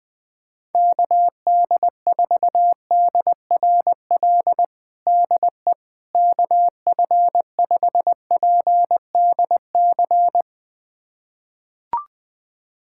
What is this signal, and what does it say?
Morse code, heard as a keyed tone, about 20 wpm, 700 Hz. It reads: KD4DRL DE KF5PDC
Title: USA Callsign DE Callsign - 6 Characters - 20wpm